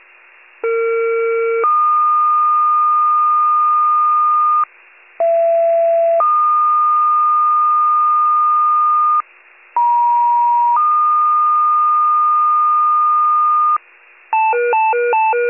Two-tone page, heard as a one-second tone, then a three-second tone, about 460 then 1220 Hz, about 670 then 1220 Hz, about 950 then 1220 Hz. It is heard from 0.6 to 13.8 s.
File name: pager/21 RESCUE